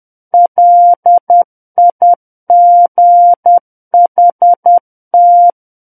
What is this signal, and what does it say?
Morse code, heard as a keyed tone, about 10 wpm, 700 Hz. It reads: LIGHT
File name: Book One / 01-Chapter 1 - 10 WPM